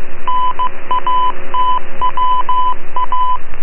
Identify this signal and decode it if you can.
Morse code, heard as a keyed tone, about 15 wpm, 1000 Hz, with radio noise. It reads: NATWA